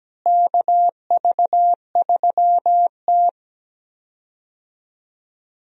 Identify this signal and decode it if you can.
Morse code, heard as a keyed tone, about 17 wpm, 700 Hz. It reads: KV3T
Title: USA Call Signs - 4 Characters - Repeated 2x - 17wpm